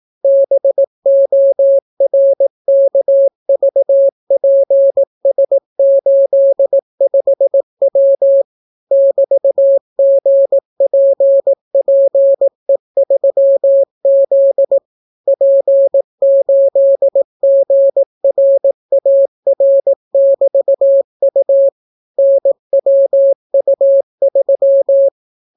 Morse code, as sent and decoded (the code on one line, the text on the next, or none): -... --- .-. -.- ...- .--. ... ---.. ..... .-- / -...- --. .--. .--. ...-- --.. / .--. ---.. --. .-. .- .-. -...- ..- / -. .-- ..- ...--
BORKVPS85W <BT>GPPE3Z P8GRAR<BT>U NWU3